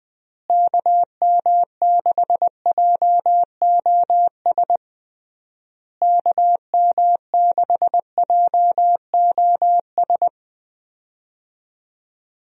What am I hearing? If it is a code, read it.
Morse code, heard as a keyed tone, about 20 wpm, 700 Hz. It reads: KM6JOS KM6JOS
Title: USA Call Signs - 6 Characters - Repeated 3x - 20wpm